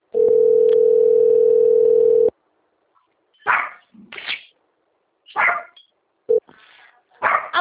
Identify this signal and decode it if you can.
Ringback tone (North America), 440 + 480 Hz.